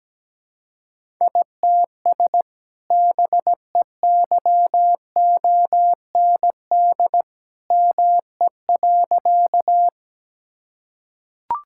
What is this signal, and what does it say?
Morse code, heard as a keyed tone, about 17 wpm, 700 Hz. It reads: ITS BEYOND ME.
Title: Sentences from Top 1000 Words - Review Part 6 - 17wpm